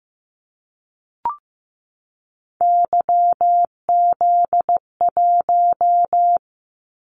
Morse code, heard as a keyed tone, about 15 wpm, 700 Hz. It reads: YZ1
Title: Callsign Prefixes for Top 25 DXCC - 15wpm